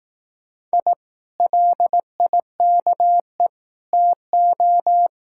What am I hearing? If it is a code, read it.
Morse code, heard as a keyed tone, about 18 wpm, 700 Hz. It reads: I LIKE TO
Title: Speed-Racing Common QSO Phrases Encore - 15wpm